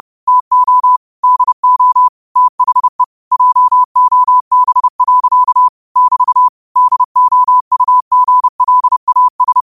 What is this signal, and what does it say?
Morse code, heard as a keyed tone, about 30 wpm, 1000 Hz. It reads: TO DO THE JOB. <BT> DOUGLAS